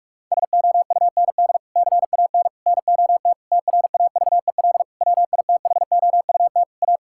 Morse code, heard as a keyed tone, about 41 wpm, 700 Hz. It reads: SOUND CAN NOT TRAVEL WITHOUT A
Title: Sentences from Top 1000 Words - Review Part 6 - 45wpm